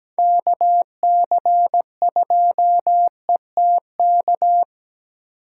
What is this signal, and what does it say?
Morse code, heard as a keyed tone, about 20 wpm, 700 Hz. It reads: KC2ETK
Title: Speed Racing - USA Call Signs 17wpm